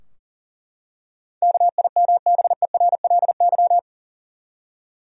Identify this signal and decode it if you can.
Morse code, heard as a keyed tone, about 40 wpm, 700 Hz. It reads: KIMBERLY
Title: USA Common Names - 40wpm